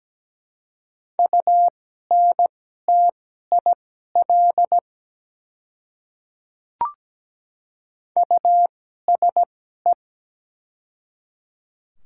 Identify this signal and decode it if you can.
Morse code, heard as a keyed tone, about 17 wpm, spaced as for 12 wpm, 700 Hz. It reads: UNTIL USE
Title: 2x Character Spacing - Top 300 Words - 17wpm